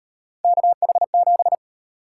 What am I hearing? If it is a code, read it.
Morse code, heard as a keyed tone, about 37 wpm, 700 Hz. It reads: KH7